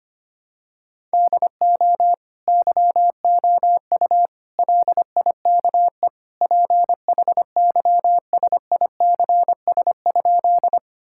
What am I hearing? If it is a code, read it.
Morse code, heard as a keyed tone, about 25 wpm, 700 Hz. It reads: DO YOU LIKE PHYSICS?
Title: Sentences from Top 1000 Words - Review Part 5 - 25wpm